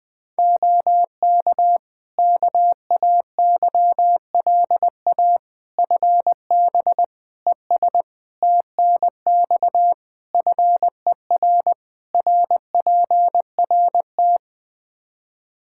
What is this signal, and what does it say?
Morse code, heard as a keyed tone, about 21 wpm, 700 Hz. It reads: OK KAYLA FB ES TNX FER RPRT
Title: K1USN SST QSO Practice 20wpm